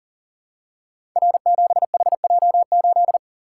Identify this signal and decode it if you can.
Morse code, heard as a keyed tone, about 40 wpm, 700 Hz. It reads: R7HJ8
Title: Code Groups - 5 Letter:Numbers - 40wpm